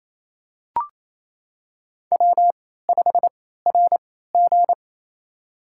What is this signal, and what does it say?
Morse code, heard as a keyed tone, about 28 wpm, spaced as for 16 wpm, 700 Hz. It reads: W5RG
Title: USA Call Signs - 3x Character Spacing - 28wpm